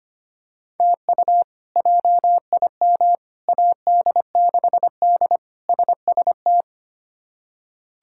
Morse code, heard as a keyed tone, about 25 wpm, 700 Hz. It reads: TU JIM AD6D SST
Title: K1USN SST QSO Practice - 25wpm